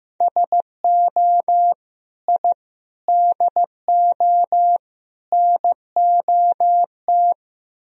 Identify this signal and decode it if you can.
Morse code, heard as a keyed tone, about 17 wpm, 700 Hz. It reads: SO I DO NOT
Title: Speed-Racing - Sets of 4 Words 15wpm